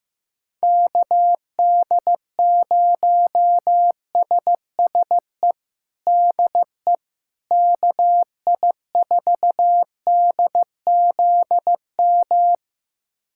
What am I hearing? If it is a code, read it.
Morse code, heard as a keyed tone, about 15 wpm, 700 Hz. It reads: KD0SSE DE KI4DZM
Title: USA Callsign DE Callsign 15wpm